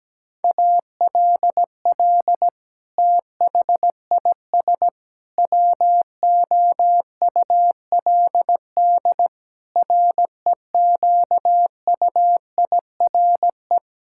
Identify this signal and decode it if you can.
Morse code, heard as a keyed tone, about 17 wpm, 700 Hz. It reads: ALL THIS WOULD REQUIRE
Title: Sentences from Top 1000 Words - Review Part 6 - 17wpm